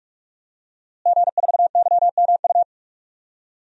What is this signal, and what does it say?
Morse code, heard as a keyed tone, about 46 wpm, 700 Hz. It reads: G4YMU
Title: DXCC-100 Call Signs - 5 Characters - Repeated 2x - 45wpm